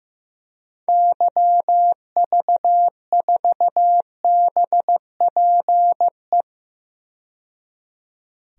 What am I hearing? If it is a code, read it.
Morse code, heard as a keyed tone, about 15 wpm, 700 Hz. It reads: YV4BPE
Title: DXCC-100 Call Signs - 6 Characters - 15wpm